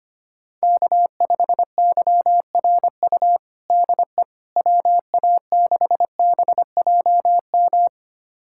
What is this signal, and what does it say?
Morse code, heard as a keyed tone, about 25 wpm, 700 Hz. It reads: K5YRU DE WA6BJM
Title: USA Callsign DE Callsign 25wpm